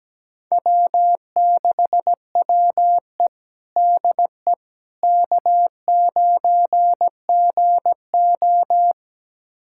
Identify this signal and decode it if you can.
Morse code, heard as a keyed tone, about 17 wpm, 700 Hz. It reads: W6WE DE K9GO
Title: USA Callsign DE Callsign - 4 Characters - 17wpm